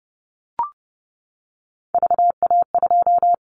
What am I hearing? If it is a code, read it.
Morse code, heard as a keyed tone, about 30 wpm, 700 Hz. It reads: VA2
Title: Callsign Prefixes for Top 50 DXCC - 30wpm